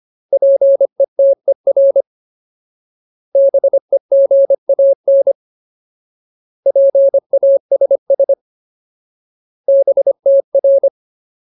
Morse code, heard as a keyed tone, about 25 wpm, 550 Hz. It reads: PETER BEGAN PASS BTR
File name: Ultimate Word List / 25WPM Ultimate